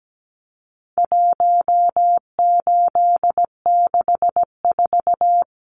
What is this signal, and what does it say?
Morse code, heard as a keyed tone, about 17 wpm, 700 Hz. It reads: 1864
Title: Numbers - 4-Digit 17wpm